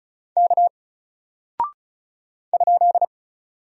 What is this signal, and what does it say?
Morse code, heard as a keyed tone, about 36 wpm, 700 Hz. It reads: K ?
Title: Single Character (Letter:Number:Punctuation:Prosign) Alt 35wpm